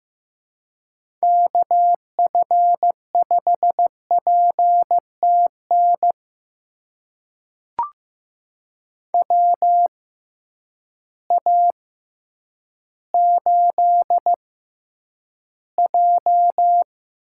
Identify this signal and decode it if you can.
Morse code, heard as a keyed tone, about 15 wpm, spaced as for 9 wpm, 700 Hz. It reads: KF5PTN WA8J
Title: USA Call Signs - 6x Character Spacing - 15wpm